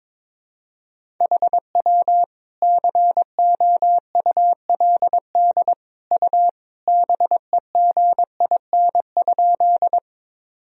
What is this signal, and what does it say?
Morse code, heard as a keyed tone, about 22 wpm, 700 Hz. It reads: HW COULD U BEGIN?
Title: Sentences from Top 300 Words and Abbreviations 22wpm